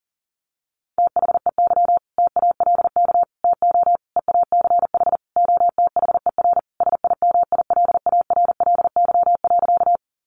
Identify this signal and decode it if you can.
Morse code, heard as a keyed tone, about 40 wpm, 700 Hz. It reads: THEY TALK TO EACH OTHER SIMILARLY.
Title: Sentences from Top 1400 Words 40wpm